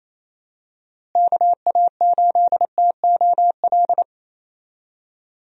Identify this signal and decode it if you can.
Morse code, heard as a keyed tone, about 27 wpm, 700 Hz. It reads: KA8TOL
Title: USA Call Signs - 6 Characters - Repeated 3x - 28wpm